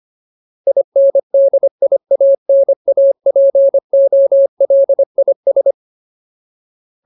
Morse code, heard as a keyed tone, about 25 wpm, 550 Hz. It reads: INDIANAPOLIS
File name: Ultimate Word List / 25WPM Ultimate